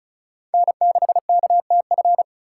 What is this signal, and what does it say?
Morse code, heard as a keyed tone, about 36 wpm, 700 Hz. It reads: N6KTF
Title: USA Call Signs - 6x Character Spacing - 35wpm